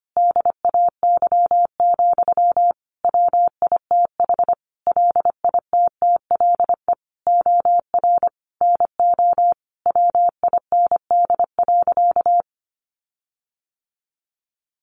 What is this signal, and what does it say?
Morse code, heard as a keyed tone, about 25 wpm, 700 Hz. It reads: DAY, WITH LITTLE OR NO WIND.
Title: Sentences from Top 1400 Words 25wpm